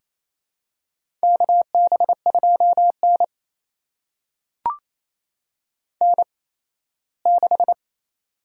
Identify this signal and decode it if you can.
Morse code, heard as a keyed tone, about 28 wpm, spaced as for 17 wpm, 700 Hz. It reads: KB2N N6
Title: USA Call Signs - 4 Characters - 8x Character Spacing - 28wpm